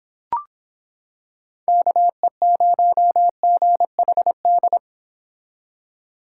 Morse code, heard as a keyed tone, about 26 wpm, 700 Hz. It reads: KE0GHD